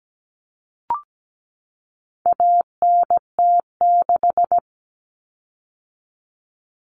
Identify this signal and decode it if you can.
Morse code, heard as a keyed tone, about 17 wpm, 700 Hz. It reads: ANT6
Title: Common Cut Numbers - 4-Digit 17wpm